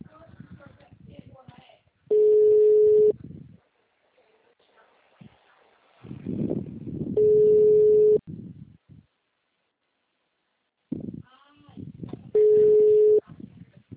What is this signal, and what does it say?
Ringback tone (Europe, 425 Hz), with speech.
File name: out- -2024- - - .81696